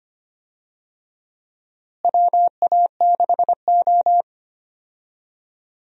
Morse code, heard as a keyed tone, about 25 wpm, 700 Hz. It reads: WA6O
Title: USA Call Signs - 4 Characters - Repeated 2x - 25wpm